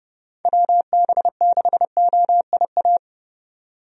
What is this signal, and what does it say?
Morse code, heard as a keyed tone, about 30 wpm, 700 Hz. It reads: WB6OIA
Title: USA Call Signs - 6 Characters - 30wpm